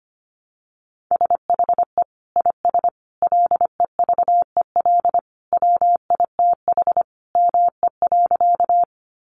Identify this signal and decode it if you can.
Morse code, heard as a keyed tone, about 25 wpm, 700 Hz. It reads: SHE IS LEVEL WITH ME.
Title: Sentences from Top 700 Words 25wpm